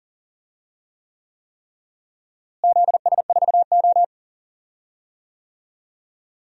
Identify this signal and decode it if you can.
Morse code, heard as a keyed tone, about 40 wpm, 700 Hz. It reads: ZS4O
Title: DXCC-100 Call Signs - 4 Characters - Repeated 3x - 40wpm